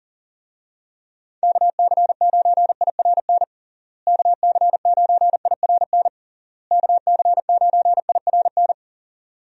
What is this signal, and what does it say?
Morse code, heard as a keyed tone, about 40 wpm, 700 Hz. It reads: KC9IRN KC9IRN KC9IRN
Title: USA Call Signs - 6 Characters - Repeated 3x - 40wpm